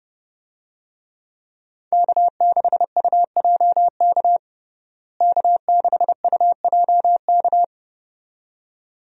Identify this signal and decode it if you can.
Morse code, heard as a keyed tone, about 30 wpm, 700 Hz. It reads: K6UJK K6UJK
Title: USA Call Signs - 5 Characters - Repeated 2x - 30wpm